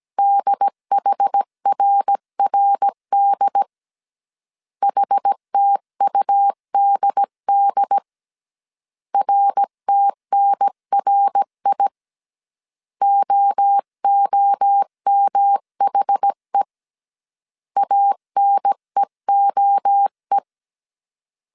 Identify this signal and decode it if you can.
Morse code, heard as a keyed tone, about 17 wpm, 800 Hz. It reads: DHRRD HTUDD RTNRI OOMHE ANEOE
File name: CW / DET / 1 3 1500C S A 16wpm 2141